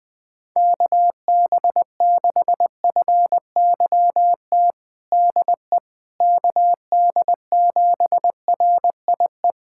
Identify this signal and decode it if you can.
Morse code, heard as a keyed tone, about 21 wpm, 700 Hz. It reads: KB6FYT DE KD7RIE
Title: USA Callsign DE Callsign - 6 Characters - 20wpm